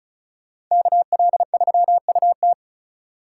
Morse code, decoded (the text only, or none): KL3UT